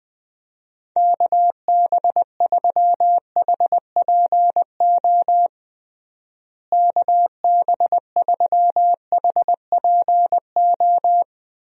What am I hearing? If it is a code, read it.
Morse code, heard as a keyed tone, about 20 wpm, 700 Hz. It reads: KB3HPO KB3HPO
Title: USA Call Signs - 6 Characters - Repeated 3x - 20wpm